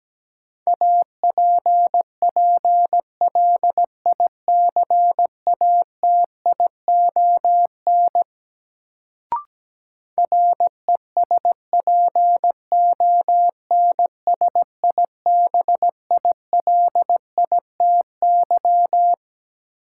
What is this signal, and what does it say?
Morse code, heard as a keyed tone, about 17 wpm, 700 Hz. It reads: APPLICATION RESPONSIBILITY
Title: Sentences from Top 1200 Words - New Word Drill - 17wpm